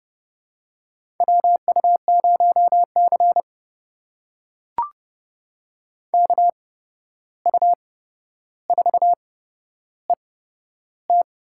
Morse code, heard as a keyed tone, about 32 wpm, spaced as for 12 wpm, 700 Hz. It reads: WU0C KU4ET
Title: USA Call Signs - 8x Character Spacing - 30wpm